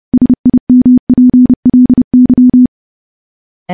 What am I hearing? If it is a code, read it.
Morse code, heard as a keyed tone, about 30 wpm, 250 Hz. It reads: SIMPLY
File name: morse / morse-training-30WPM